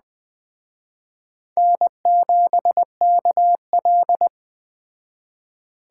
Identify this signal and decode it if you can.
Morse code, heard as a keyed tone, about 20 wpm, 700 Hz. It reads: N7KL